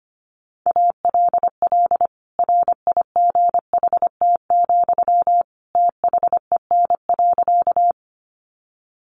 Morse code, heard as a keyed tone, about 25 wpm, 700 Hz. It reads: ALL RIGHT, THEN.
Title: Sentences from Top 200 Words 25wpm